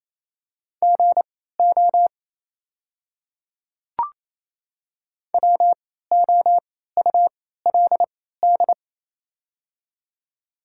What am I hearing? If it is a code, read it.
Morse code, heard as a keyed tone, about 28 wpm, spaced as for 16 wpm, 700 Hz. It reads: GO WOULD
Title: 3x Character Spacing - Top 100 Words - 28wpm